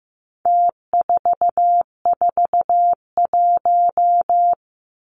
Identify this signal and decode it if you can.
Morse code, heard as a keyed tone, about 15 wpm, 700 Hz. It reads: T441
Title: Numbers - Contest Style - 4-Digit 15wpm